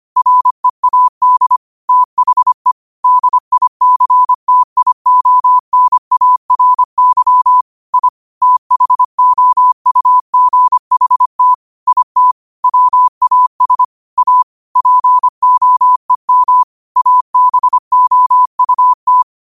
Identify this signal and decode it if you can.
Morse code, heard as a keyed tone, about 25 wpm, 1000 Hz. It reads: READ THE DICTIONARY I THOUGHT IT WAS A POEM ABOUT
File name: qotdfiles / 25wpm